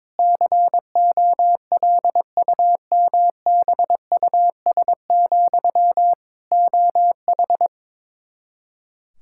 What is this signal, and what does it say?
Morse code, heard as a keyed tone, about 22 wpm, 700 Hz. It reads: COLUMBUS, OH
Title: K1USN SST QSO Practice 22wpm